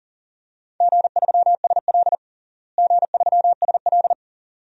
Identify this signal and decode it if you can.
Morse code, heard as a keyed tone, about 40 wpm, 700 Hz. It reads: G3SL G3SL
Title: DXCC-100 Call Signs - 4 Characters - Repeated 2x - 40wpm